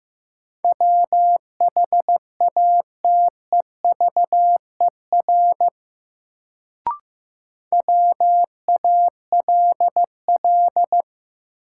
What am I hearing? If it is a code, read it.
Morse code, heard as a keyed tone, about 16 wpm, 700 Hz. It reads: WHATEVER WALL
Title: Sentences from Top 1100 Words - New Word Drill Alt - 15wpm